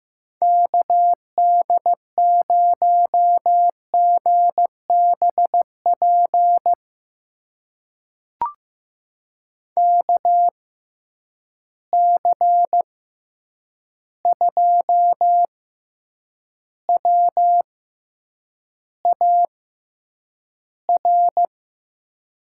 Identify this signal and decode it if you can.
Morse code, heard as a keyed tone, about 15 wpm, spaced as for 8 wpm, 700 Hz. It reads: KD0GBP KC2WAR
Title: USA Call Signs - 6 Characters - 6x Character Spacing - 15wpm